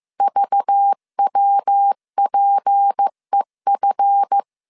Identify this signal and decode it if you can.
Morse code, heard as a keyed tone, about 15 wpm, 800 Hz. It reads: VWPEF
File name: CW / DET / 1 9 1500C S B 14wpm 3302